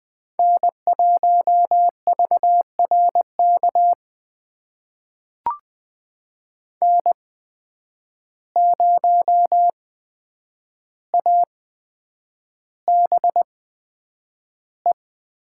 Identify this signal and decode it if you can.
Morse code, heard as a keyed tone, about 20 wpm, spaced as for 9 wpm, 700 Hz. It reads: N1VRK N0ABE
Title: USA Call Signs - 5 Characters - 8x Character Spacing - 20wpm